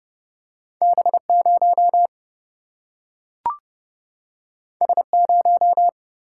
Morse code, heard as a keyed tone, about 30 wpm, 700 Hz. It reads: B 0 S 0
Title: Single-Letter Single-Number 30wpm